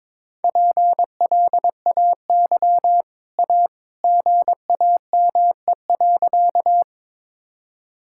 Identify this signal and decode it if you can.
Morse code, heard as a keyed tone, about 21 wpm, 690 Hz. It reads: PLAY A GAME.
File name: practice / 22wpm / Easy Sentences with Hints - 22wpm